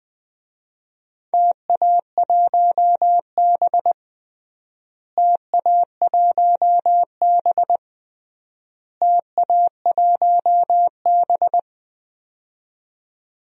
Morse code, heard as a keyed tone, about 21 wpm, 700 Hz. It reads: TA1B TA1B TA1B
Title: DXCC-100 Call Signs - 4 Characters - Repeated 3x - 20wpm